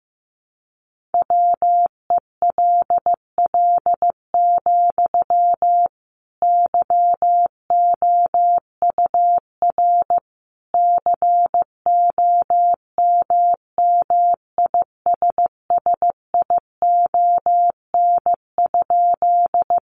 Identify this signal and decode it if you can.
Morse code, heard as a keyed tone, about 15 wpm, 700 Hz. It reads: WELL, YOUR COMMISSION?
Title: Sentences from Top 1400 Words 15wpm